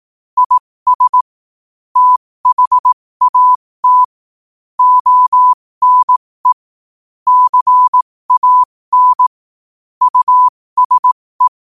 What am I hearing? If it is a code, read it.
Morse code, heard as a keyed tone, about 18 wpm, spaced as for 15 wpm, 1000 Hz. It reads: IS THAT ONE CAN USE
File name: qotdfiles / 15wpm